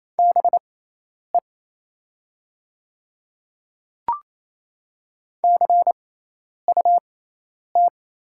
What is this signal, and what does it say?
Morse code, heard as a keyed tone, about 28 wpm, spaced as for 10 wpm, 700 Hz. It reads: BE CUT
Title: 6x Character Spacing - Top 500 Words - 28wpm